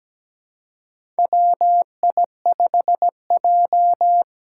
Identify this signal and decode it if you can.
Morse code, heard as a keyed tone, about 17 wpm, 700 Hz. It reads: WI5J
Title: Four Challenging Letter-Number Combinations 17wpm